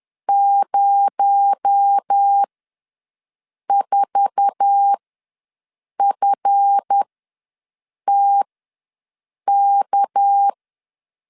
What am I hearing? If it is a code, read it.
Morse code, heard as a keyed tone, about 10 wpm, spaced as for 6 wpm, 800 Hz. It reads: 04FTK
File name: LEZIONI / 8 YJ45 / 8 YJ4 ws 10wpm 4020